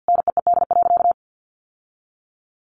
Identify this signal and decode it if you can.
Morse code, heard as a keyed tone, about 49 wpm, 700 Hz. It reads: NEED.
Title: Sentences from Top 1100 Words 50wpm